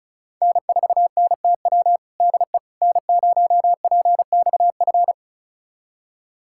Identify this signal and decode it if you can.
Morse code, heard as a keyed tone, about 36 wpm, 690 Hz. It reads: N4NTW DE N0PXF
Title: USA Callsign DE Callsign - 5 Characters - 35wpm